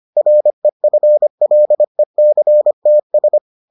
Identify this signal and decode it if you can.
Morse code, heard as a keyed tone, about 25 wpm, 600 Hz. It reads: REFLECTS